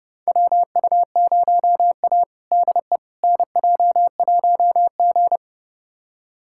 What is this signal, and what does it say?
Morse code, heard as a keyed tone, about 30 wpm, 700 Hz. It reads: WU0A DE NJ1G
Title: USA Callsign DE Callsign - 4 Characters - 30wpm